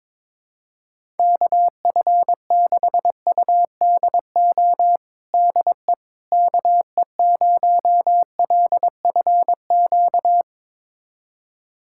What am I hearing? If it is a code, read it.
Morse code, heard as a keyed tone, about 22 wpm, 700 Hz. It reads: KF6UDO DE KE0LFQ
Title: USA Callsign DE Callsign 22wpm